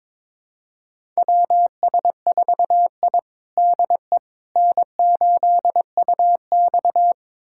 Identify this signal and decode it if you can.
Morse code, heard as a keyed tone, about 22 wpm, 700 Hz. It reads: WS4I DE N8UX
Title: USA Callsign DE Callsign - 4 Characters - 22wpm